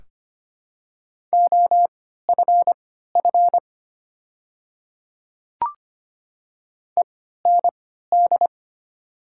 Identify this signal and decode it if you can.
Morse code, heard as a keyed tone, about 25 wpm, spaced as for 14 wpm, 690 Hz. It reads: OFF END
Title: 3x Character Spacing - Top 200 Words - 25wpm